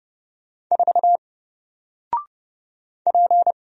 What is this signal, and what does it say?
Morse code, heard as a keyed tone, about 30 wpm, 700 Hz. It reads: 4 P
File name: Single-Letter-Number-30wpm